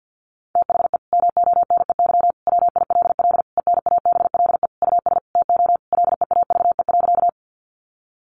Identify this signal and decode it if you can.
Morse code, heard as a keyed tone, about 50 wpm, 700 Hz. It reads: THE MONEY WILL ENABLE US TO LEAVE.
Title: Sentences from Top 1500 Words 50wpm